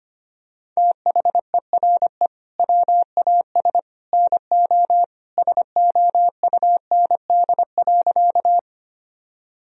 Morse code, heard as a keyed tone, about 25 wpm, 700 Hz. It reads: THERE WAS NO SOUND.
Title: Sentences from Top 1000 Words - Review Part 1 - 25wpm